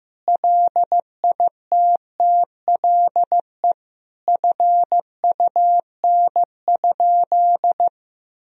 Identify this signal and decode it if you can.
Morse code, heard as a keyed tone, about 15 wpm, 700 Hz. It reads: LITTLE FUN?
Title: Sentences from Top 1000 Words - Review Part 1 - 15wpm